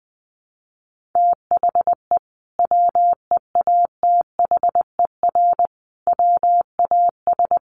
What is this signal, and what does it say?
Morse code, heard as a keyed tone, about 19 wpm, 700 Hz. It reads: THE WEATHER WAS
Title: Sentences from Top 1400 Words 20wpm